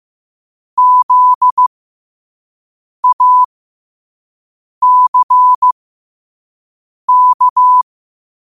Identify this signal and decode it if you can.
Morse code, heard as a keyed tone, about 15 wpm, spaced as for 5 wpm, 1000 Hz. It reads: ZACK
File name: qotdfiles / 5wpm